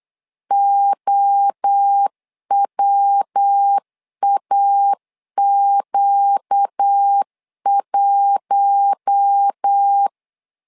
Morse code, heard as a keyed tone, about 8 wpm, 800 Hz. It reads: OWAQ1
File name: LEZIONI / 4 Q1 8wpm 3035